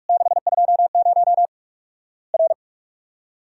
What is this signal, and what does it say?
Morse code, heard as a keyed tone, about 44 wpm, 700 Hz. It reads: 620 R
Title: Worked All Europe DX Contest - QTC Exchange - 45wpm